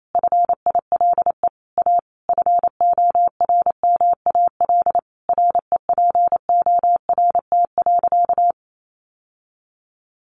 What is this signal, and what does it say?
Morse code, heard as a keyed tone, about 28 wpm, 700 Hz. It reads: FILE A FORMAL REPORT.
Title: Sentences from Top 1300 Words 28wpm